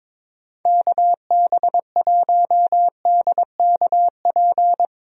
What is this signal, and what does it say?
Morse code, heard as a keyed tone, about 22 wpm, 700 Hz. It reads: KB1DKP
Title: USA Call Signs - 6 Characters - 22wpm